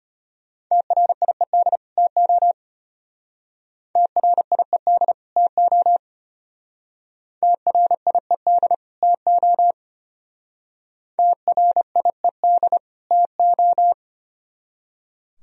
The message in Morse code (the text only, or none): TRIED TO TRIED TO TRIED TO TRIED TO